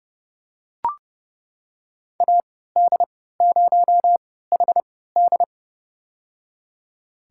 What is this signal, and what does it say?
Morse code, heard as a keyed tone, about 30 wpm, spaced as for 17 wpm, 700 Hz. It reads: AD0HD
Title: USA Call Signs - 5 Characters - 3x Character Spacing - 30wpm